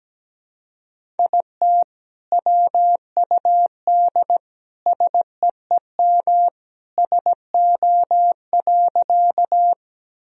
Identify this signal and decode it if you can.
Morse code, heard as a keyed tone, about 17 wpm, 700 Hz. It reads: IT WUD SEEM SO.
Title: Sentences from Top 300 Words and Abbreviations 17wpm